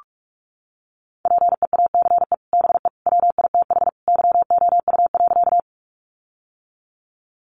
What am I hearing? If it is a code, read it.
Morse code, heard as a keyed tone, about 44 wpm, 700 Hz. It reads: PEACE BE WITH YOU.
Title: Sentences from Top 1500 Words 45wpm